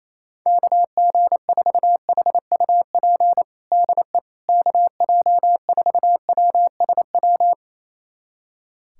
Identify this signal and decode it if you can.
Morse code, heard as a keyed tone, about 28 wpm, 700 Hz. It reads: KG4HUP DE KJ4WSW
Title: USA Callsign DE Callsign - 6 Characters - 28wpm